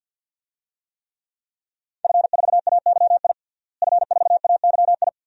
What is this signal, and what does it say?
Morse code, heard as a keyed tone, about 52 wpm, 700 Hz. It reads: F4AYI F4AYI
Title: DXCC-100 Call Signs - 5 Characters - Repeated 2x - 50wpm